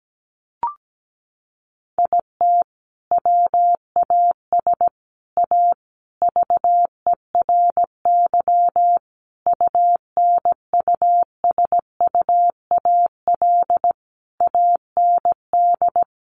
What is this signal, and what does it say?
Morse code, heard as a keyed tone, about 17 wpm, 700 Hz. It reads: IT WAS A VERY UNUSUAL AND
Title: Sentences from Top 1400 Words 17wpm